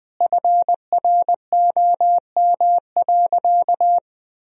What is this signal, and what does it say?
Morse code, heard as a keyed tone, about 20 wpm, 700 Hz. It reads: FROM.
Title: Sentences from Top 1000 Words - Review Part 4 - 20wpm